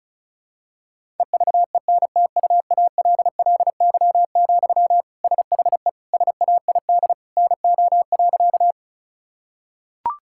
Morse code, heard as a keyed tone, about 34 wpm, 700 Hz. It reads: EVENTUALLY, SHE SAID NO.
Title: Sentences from Top 1000 Words - Review Part 6 - 35wpm